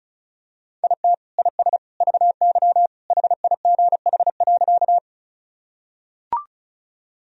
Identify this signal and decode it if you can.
Morse code, heard as a keyed tone, about 36 wpm, 700 Hz. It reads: IT IS VY HIGH.
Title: Sentences from Top 200 Words and Abbreviations 35wpm